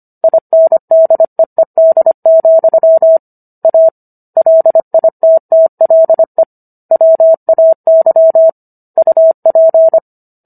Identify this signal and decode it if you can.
Morse code, heard as a keyed tone, about 25 wpm, 650 Hz. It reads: INDEED, A LITTLE WAY UP